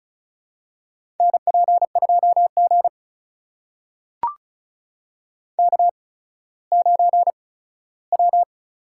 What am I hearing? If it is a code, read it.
Morse code, heard as a keyed tone, about 36 wpm, spaced as for 17 wpm, 700 Hz. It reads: NP2G K9W
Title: USA Call Signs - 4 Characters - 8x Character Spacing - 35wpm